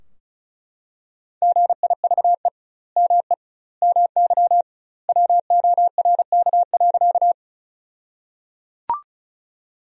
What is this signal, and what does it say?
Morse code, heard as a keyed tone, about 34 wpm, 700 Hz. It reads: GIVE ME MY WORK.
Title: 2x Word-Spacing - Sentences from Top 200 Words - 35wpm